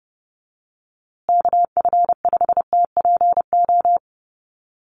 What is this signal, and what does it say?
Morse code, heard as a keyed tone, about 30 wpm, 700 Hz. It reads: KF5TPO
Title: USA Call Signs Repeated 2x 30wpm